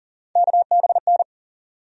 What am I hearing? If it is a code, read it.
Morse code, heard as a keyed tone, about 40 wpm, 700 Hz. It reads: KBN
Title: USA Call Sign Postfixes - 3 Letters 40wpm